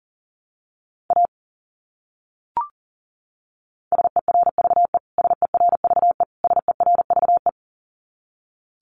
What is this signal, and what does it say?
Morse code, heard as a keyed tone, about 40 wpm, 700 Hz. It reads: A SERVE SERVE SERVE
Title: Top 500 Words Repeated 40wpm